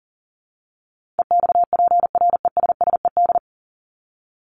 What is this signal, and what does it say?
Morse code, heard as a keyed tone, about 40 wpm, 700 Hz. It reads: EXPRESSED
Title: ED Suffix - Top 100 Words with Base Words 40wpm